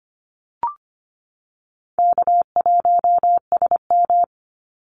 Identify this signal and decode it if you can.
Morse code, heard as a keyed tone, about 25 wpm, 700 Hz. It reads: K1SM
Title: Easy Call Signs Repeated 3x - 25wpm